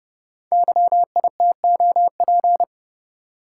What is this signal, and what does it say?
Morse code, heard as a keyed tone, about 31 wpm, 700 Hz. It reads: YITOP
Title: Code Groups - 5 Letters - 30wpm